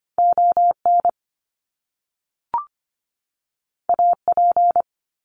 Morse code, heard as a keyed tone, about 25 wpm, 700 Hz. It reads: ON AP
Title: Top Two Letter Combinations in Words 25wpm